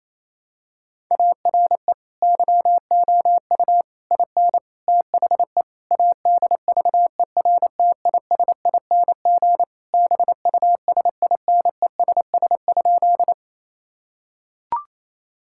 Morse code, heard as a keyed tone, about 28 wpm, 700 Hz. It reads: ARE YOU IN THE ADVERTISING BUSINESS?
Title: Sentences from Top 1000 Words - Review Part 6 - 28wpm